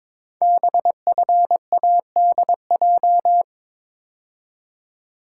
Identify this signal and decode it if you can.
Morse code, heard as a keyed tone, about 22 wpm, 700 Hz. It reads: BFADJ
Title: Code Groups - 5 Letters - 22wpm